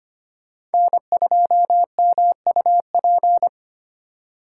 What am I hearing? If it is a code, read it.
Morse code, heard as a keyed tone, about 32 wpm, 700 Hz. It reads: N2MUP